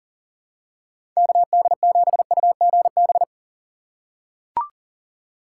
Morse code, heard as a keyed tone, about 40 wpm, 700 Hz. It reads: KD7UGB
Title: USA Call Signs - 6 Characters - 3x Character Spacing - 40wpm